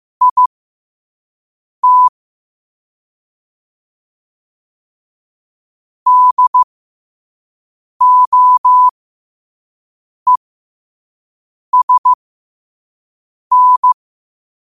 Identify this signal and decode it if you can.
Morse code, heard as a keyed tone, about 15 wpm, spaced as for 5 wpm, 1000 Hz. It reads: IT DOESN
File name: qotdfiles / 5wpm